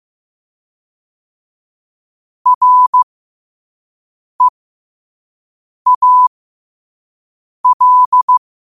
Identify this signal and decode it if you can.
Morse code, heard as a keyed tone, about 15 wpm, spaced as for 5 wpm, 1000 Hz. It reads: REAL
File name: qotdfiles / 5wpm